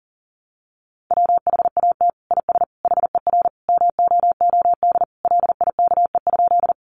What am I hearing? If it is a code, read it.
Morse code, heard as a keyed tone, about 40 wpm, 700 Hz. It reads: WHAT IS HER MOOD LIKE?
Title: Sentences from Top 1100 Words 40wpm